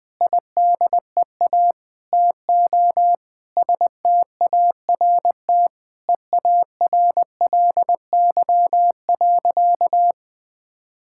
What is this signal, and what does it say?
Morse code, heard as a keyed tone, about 20 wpm, 700 Hz. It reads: IDEA TO START EARLY.
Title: Sentences with 8-Word Phrases - 20wpm